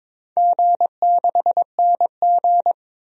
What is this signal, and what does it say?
Morse code, heard as a keyed tone, about 22 wpm, 700 Hz. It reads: G6NG
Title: DXCC-100 Call Signs - 4 Characters - 22wpm